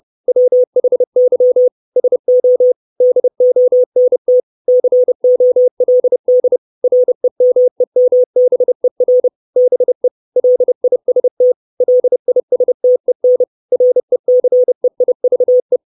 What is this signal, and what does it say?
Morse code, heard as a keyed tone, about 30 wpm, 500 Hz. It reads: WHY SO DONT COLD REMEMBER BE LIST LISTEN RECEIVE
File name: 632 Words 30WPM